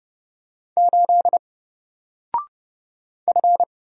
Morse code, heard as a keyed tone, about 30 wpm, 700 Hz. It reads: 8 F
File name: Single-Letter-Number-30wpm